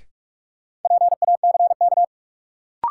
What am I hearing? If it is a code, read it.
Morse code, heard as a keyed tone, about 43 wpm, 700 Hz. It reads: PACK
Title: Sentences from Top 1300 Words - New Word Drill Alt - 45wpm